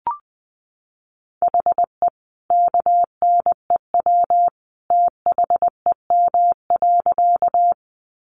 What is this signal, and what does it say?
Morse code, heard as a keyed tone, about 20 wpm, 700 Hz. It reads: HE KNEW THEM.
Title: Sentences from Top 400 Words 20wpm